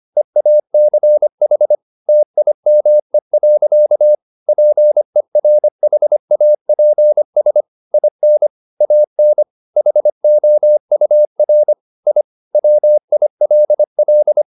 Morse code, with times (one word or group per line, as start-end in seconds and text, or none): 0.2-1.8 EACH
2.1-4.2 TIME.
4.5-7.6 PERHAPS
7.9-8.5 IN
8.8-9.4 AN
9.8-11.7 HOUR
12.1-12.2 I
12.6-14.4 WILL